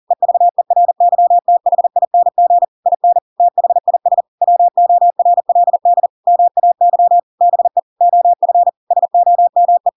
Morse code, heard as a keyed tone, about 40 wpm, 700 Hz. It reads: EVERYTHING IN THIS WORLD MAY BE OF SOME
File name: Pinocchio / Pinocchio ch7 40 40 ik1hge